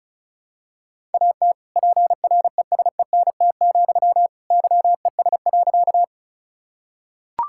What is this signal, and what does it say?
Morse code, heard as a keyed tone, about 34 wpm, 700 Hz. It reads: AT PRESENT, YES.
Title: Sentences from Top 1000 Words - Review Part 5 - 35wpm